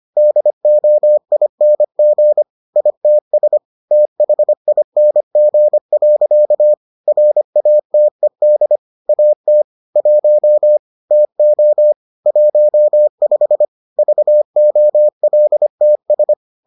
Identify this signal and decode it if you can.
Morse code, heard as a keyed tone, about 25 wpm, 600 Hz. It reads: DOING ITS THING. RATED AT 1 TO 15 VOLTS